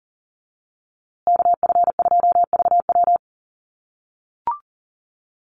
Morse code, heard as a keyed tone, about 40 wpm, 700 Hz. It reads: KF2VW
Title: USA Call Signs Repeated 3x 40wpm